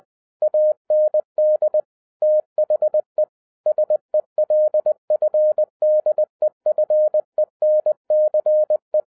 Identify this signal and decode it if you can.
Morse code, heard as a keyed tone, about 20 wpm, 610 Hz. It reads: AND THE SELF DEFENCE